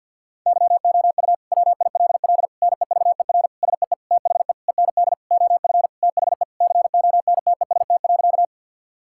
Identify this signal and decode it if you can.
Morse code, heard as a keyed tone, about 52 wpm, 700 Hz. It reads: YOU WILL NEVER SEE THE END OF THE CONTEST.
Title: Sentences from Top 1600 Words 50wpm